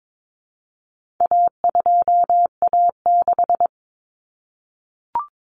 Morse code, heard as a keyed tone, about 22 wpm, 700 Hz. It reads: A2A6
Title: Common Cut Numbers - 4-Digit 22wpm